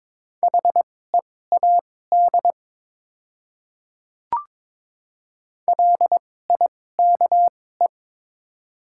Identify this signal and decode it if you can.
Morse code, heard as a keyed tone, about 22 wpm, spaced as for 16 wpm, 690 Hz. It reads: HEAD LIKE
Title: 2x Character Spacing - Top 500 Words - 22wpm